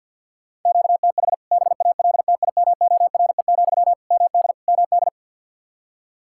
Morse code, heard as a keyed tone, about 50 wpm, 700 Hz. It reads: QTH BALTIMORE, MD MD
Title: K1USN SST QSO Practice 50wpm